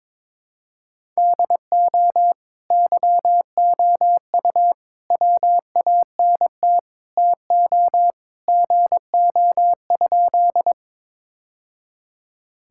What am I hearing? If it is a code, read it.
Morse code, heard as a keyed tone, about 22 wpm, 700 Hz. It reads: DO YOU WANT TO GO?
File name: practice / 22wpm / Sentences with 3-Word Phrases - 22wpm